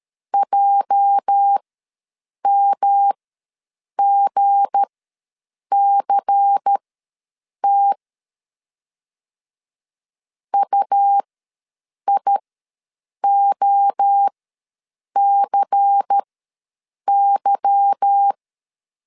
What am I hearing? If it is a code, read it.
Morse code, heard as a keyed tone, about 13 wpm, spaced as for 7 wpm, 800 Hz. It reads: JMGCT UIOCY
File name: LEZIONI / 8 YJ45 / 8 YJ4 ws 12wpm 3249